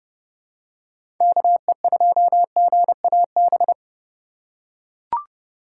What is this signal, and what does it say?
Morse code, heard as a keyed tone, about 30 wpm, 700 Hz. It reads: KE2GAB
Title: USA Call Signs - 6 Characters - 4x Character Spacing - 30wpm